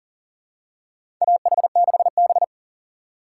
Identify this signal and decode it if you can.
Morse code, heard as a keyed tone, about 40 wpm, 700 Hz. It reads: AH6B
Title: USA Call Signs - 4 Characters - 2x Character Spacing - 40wpm